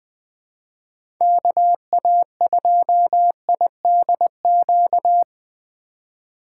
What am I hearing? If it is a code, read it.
Morse code, heard as a keyed tone, about 20 wpm, 700 Hz. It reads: KA2IDQ
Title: USA Call Signs - 4x Character Spacing - 20wpm